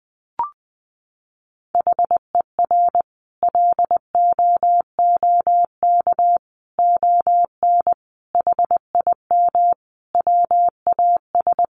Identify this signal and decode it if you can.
Morse code, heard as a keyed tone, about 20 wpm, 700 Hz. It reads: HER LOOK ON HIM WAS
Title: Sentences from Top 1400 Words 20wpm